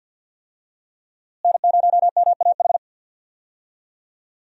Morse code, heard as a keyed tone, about 52 wpm, 700 Hz. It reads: N0MAH
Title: USA Call Signs - 5 Characters - Repeated 3x - 50wpm